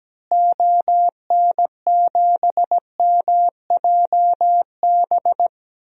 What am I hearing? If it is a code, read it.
Morse code, heard as a keyed tone, about 17 wpm, 700 Hz. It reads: ON7MJB
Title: DXCC-100 Call Signs - 6 Characters - Repeated 3x - 17wpm